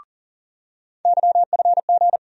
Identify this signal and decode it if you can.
Morse code, heard as a keyed tone, about 40 wpm, 700 Hz. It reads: YFG
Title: USA Call Sign Postfixes - 3 Letters 40wpm